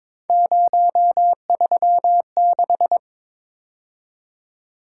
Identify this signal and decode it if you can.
Morse code, heard as a keyed tone, about 25 wpm, 700 Hz. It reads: 036
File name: Speed-Racing - Three Letter-Number Combinations - 22wpm